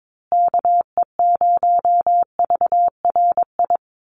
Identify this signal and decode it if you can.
Morse code, heard as a keyed tone, about 22 wpm, 700 Hz. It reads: KE0VRI
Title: USA Call Signs Repeated 2x 22wpm